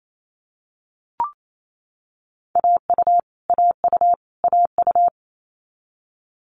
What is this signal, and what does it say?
Morse code, heard as a keyed tone, about 28 wpm, 700 Hz. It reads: AU AU AU
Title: Multi-letter Phonemes Repeated 3x 28wpm